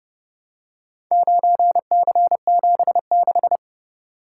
Morse code, heard as a keyed tone, about 29 wpm, 700 Hz. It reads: 9C76